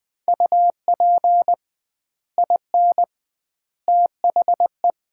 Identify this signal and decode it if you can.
Morse code, heard as a keyed tone, about 20 wpm, 700 Hz. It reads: UP IN THE
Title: 2x Word-Spacing - Sets of 3 Words - 20wpm